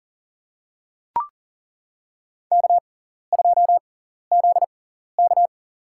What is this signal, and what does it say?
Morse code, heard as a keyed tone, about 40 wpm, spaced as for 14 wpm, 700 Hz. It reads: K2ZK
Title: USA Call Signs - 4 Characters - 6x Character Spacing - 40wpm